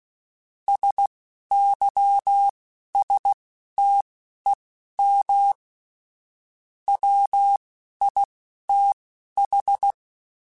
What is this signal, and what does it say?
Morse code, heard as a keyed tone, about 16 wpm, spaced as for 11 wpm, 800 Hz. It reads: SYSTEM WITH
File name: Chuckles / Chuckles-mp3s / Guaranteed @10